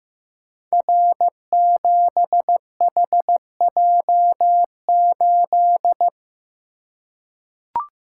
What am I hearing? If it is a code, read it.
Morse code, heard as a keyed tone, about 15 wpm, 700 Hz. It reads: R7HJ8
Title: Code Groups - 5 Letter:Numbers - 15wpm